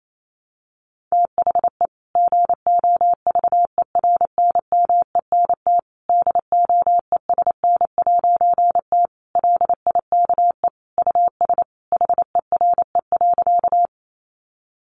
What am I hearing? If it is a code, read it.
Morse code, heard as a keyed tone, about 28 wpm, 700 Hz. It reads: THE GOVERNMENT DOESN'T LIKE US HERE.
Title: A Sound Of Thunder 28wpm